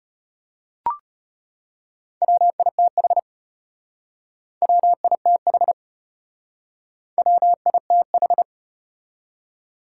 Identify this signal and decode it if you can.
Morse code, heard as a keyed tone, about 33 wpm, 700 Hz. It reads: WITH WITH WITH